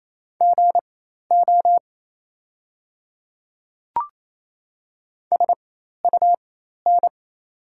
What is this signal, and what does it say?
Morse code, heard as a keyed tone, about 28 wpm, spaced as for 13 wpm, 700 Hz. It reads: GO SUN
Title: 4x Character Spacing - Top 500 Words - 28wpm